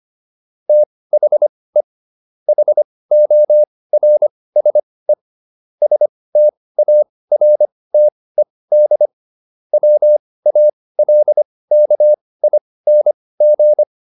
Morse code, heard as a keyed tone, about 25 wpm, spaced as for 18 wpm, 600 Hz. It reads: THE HORSE STARTED WALKING
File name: SS123 18